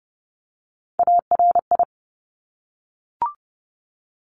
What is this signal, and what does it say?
Morse code, heard as a keyed tone, about 30 wpm, 700 Hz. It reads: ARI
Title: Top Three Letter Combinations in Words 30wpm